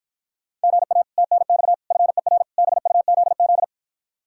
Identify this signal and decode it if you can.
Morse code, heard as a keyed tone, about 52 wpm, 700 Hz. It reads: GA TNX FER BUZZ